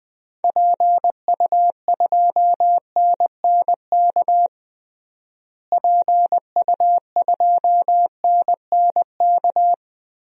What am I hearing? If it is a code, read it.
Morse code, heard as a keyed tone, about 21 wpm, 700 Hz. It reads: PU2NNK PU2NNK
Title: DXCC-100 Call Signs - 6 Characters - Repeated 2x - 20wpm